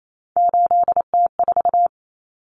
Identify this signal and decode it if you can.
Morse code, heard as a keyed tone, about 28 wpm, 700 Hz. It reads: 8T4
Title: Common Cut Numbers - 3-Digit 28wpm